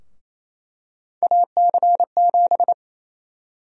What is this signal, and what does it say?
Morse code, heard as a keyed tone, about 28 wpm, 700 Hz. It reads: AC7
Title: USA Call Sign Prefixes - Repeated 2x - 28wpm